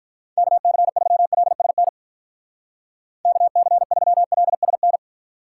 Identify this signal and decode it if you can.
Morse code, heard as a keyed tone, about 46 wpm, 700 Hz. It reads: KC3LSN KC3LSN